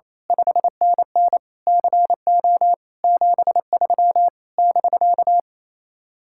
Morse code, heard as a keyed tone, about 27 wpm, 700 Hz. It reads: 5NN CO 73 <BK>
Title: Speed-Racing Common QSO Phrases - 28wpm